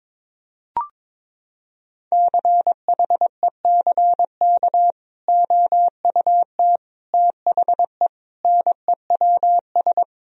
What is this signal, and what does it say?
Morse code, heard as a keyed tone, about 22 wpm, 700 Hz. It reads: CHECK OUT THE NEWS
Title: Sentences from Top 1000 Words - Review Part 4 - 22wpm